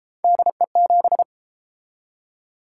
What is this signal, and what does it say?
Morse code, heard as a keyed tone, about 31 wpm, 700 Hz. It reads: DE7